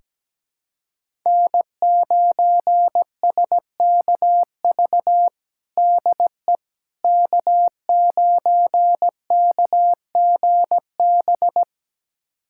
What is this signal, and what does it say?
Morse code, heard as a keyed tone, about 17 wpm, 700 Hz. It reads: N9SKV DE K9KGB
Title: USA Callsign DE Callsign - 5 Characters - 17wpm